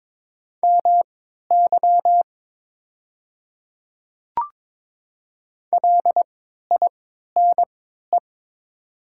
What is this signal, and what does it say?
Morse code, heard as a keyed tone, about 22 wpm, spaced as for 12 wpm, 700 Hz. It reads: MY LINE
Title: 3x Character Spacing - Top 200 Words - 22wpm